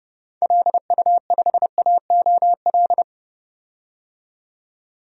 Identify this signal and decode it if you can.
Morse code, heard as a keyed tone, about 30 wpm, 700 Hz. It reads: LU5AOL
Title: DXCC-100 Call Signs - 6 Characters - 30wpm